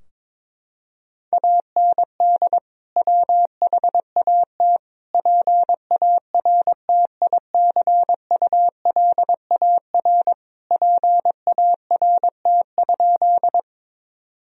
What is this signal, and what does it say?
Morse code, heard as a keyed tone, about 22 wpm, 700 Hz. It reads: AND WHAT PARTICULAR PART?
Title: Sentences from Top 1000 Words - Review Part 4 - 22wpm